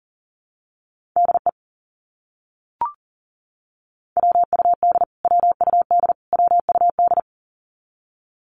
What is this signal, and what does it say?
Morse code, heard as a keyed tone, about 40 wpm, 700 Hz. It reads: DE WUD WUD WUD
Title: Abbreviations Repeated 3x - 40wpm